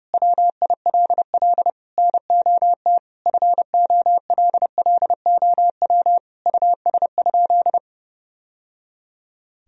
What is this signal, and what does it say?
Morse code, heard as a keyed tone, about 31 wpm, 700 Hz. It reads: WILL NOT FOLLOW US?
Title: Sentences from Top 1000 Words - Review Part 1 - 30wpm